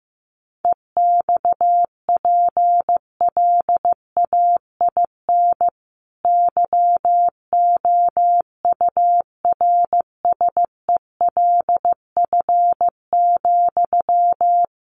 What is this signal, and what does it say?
Morse code, heard as a keyed tone, about 15 wpm, 700 Hz. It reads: EXPLAIN YOURSELF,
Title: Sentences from Top 1100 Words 15wpm